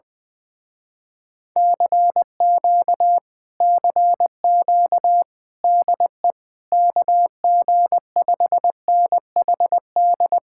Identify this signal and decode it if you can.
Morse code, heard as a keyed tone, about 20 wpm, 700 Hz. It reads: CQ CQ DE KG5NHD
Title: Speed-Racing Common QSO Phrases Encore - 20wpm